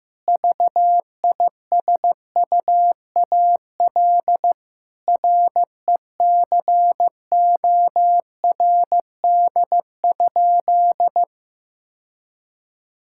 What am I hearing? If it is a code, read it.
Morse code, heard as a keyed tone, about 15 wpm, 700 Hz. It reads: VISUAL RECORD?
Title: Sentences from Top 1000 Words - Review Part 6 - 15wpm